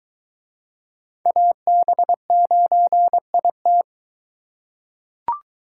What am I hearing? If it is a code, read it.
Morse code, heard as a keyed tone, about 22 wpm, 700 Hz. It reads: AB9IT